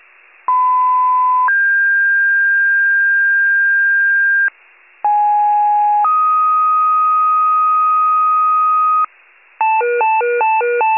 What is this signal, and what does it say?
Two-tone page, heard as a one-second tone, then a three-second tone, about 1010 then 1600 Hz, about 830 then 1250 Hz, with noise. It is heard from 0.5 to 9.1 s.